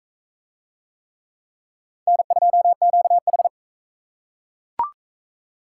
Morse code, heard as a keyed tone, about 38 wpm, 700 Hz. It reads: N2QH